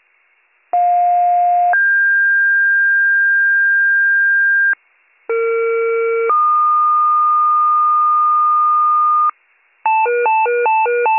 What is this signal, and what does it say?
Two-tone page, heard as a one-second tone, then a three-second tone, about 710 then 1600 Hz, about 460 then 1220 Hz, with noise. It is heard from 0.7 to 9.3 s.